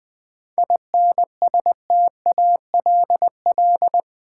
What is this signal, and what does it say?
Morse code, heard as a keyed tone, about 20 wpm, 700 Hz. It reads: INSTALL
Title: Sentences from Top 1600 Words - New Word Drill Alt - 20wpm